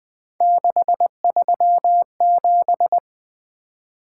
Morse code, heard as a keyed tone, about 20 wpm, 700 Hz. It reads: 637
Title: Three Letter-Number Combinations - 20wpm